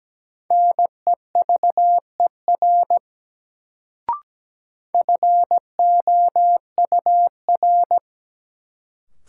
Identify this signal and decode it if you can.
Morse code, heard as a keyed tone, about 17 wpm, 700 Hz. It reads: NEVER FOUR
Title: Sentences from Top 300 Words - New Word Drill - 17wpm